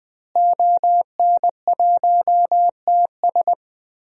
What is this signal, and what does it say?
Morse code, heard as a keyed tone, about 20 wpm, 700 Hz. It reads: ON1TS
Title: DXCC-100 Call Signs - 5 Characters - Repeated 2x - 20wpm